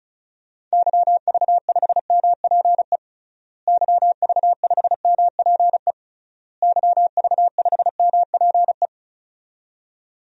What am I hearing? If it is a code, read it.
Morse code, heard as a keyed tone, about 36 wpm, 700 Hz. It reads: YV5MPE YV5MPE YV5MPE
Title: DXCC-100 Call Signs - 6 Characters - Repeated 3x - 35wpm